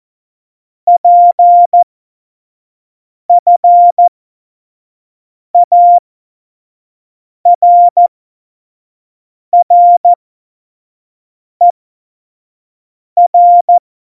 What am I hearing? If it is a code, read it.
Morse code, heard as a keyed tone, about 14 wpm, spaced as for 5 wpm, 700 Hz. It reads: PFARRER
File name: Morsekurs / Witze / Witz 08. Dezember2020, 25 F70